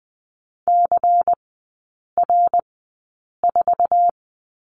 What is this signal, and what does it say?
Morse code, heard as a keyed tone, about 20 wpm, spaced as for 8 wpm, 700 Hz. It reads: CR4
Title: 093 Lesson 16e - Letter c - Three Characters 20wpm